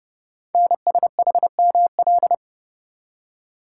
Morse code, heard as a keyed tone, about 30 wpm, 700 Hz. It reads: NSHML
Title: Code Groups - 5 Letters - 30wpm